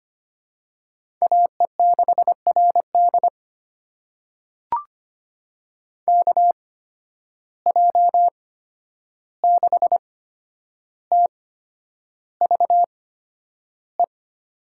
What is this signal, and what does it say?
Morse code, heard as a keyed tone, about 25 wpm, spaced as for 10 wpm, 700 Hz. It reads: AE6RD KJ6TVE
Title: USA Call Signs - 8x Character Spacing - 25wpm